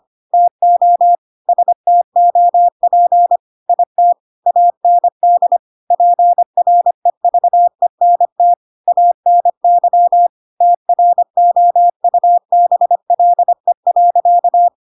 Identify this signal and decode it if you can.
Morse code, heard as a keyed tone, about 25 wpm, 700 Hz. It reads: TO STOP IT AND PREVENT ANY TROUBLE.
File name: Pinocchio / Pinocchio ch3 25 25 ik1hge